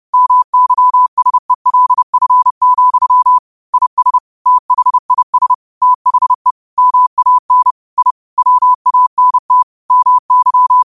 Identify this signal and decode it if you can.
Morse code, heard as a keyed tone, about 30 wpm, 1000 Hz. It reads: MYSELF, IS THIS THE MAN I WANT MY
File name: qotdfiles / 30wpm